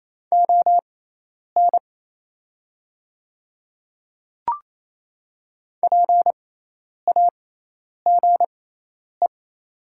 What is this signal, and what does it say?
Morse code, heard as a keyed tone, about 28 wpm, spaced as for 10 wpm, 700 Hz. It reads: ON PAGE
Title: 6x Character Spacing - Top 300 Words - 28wpm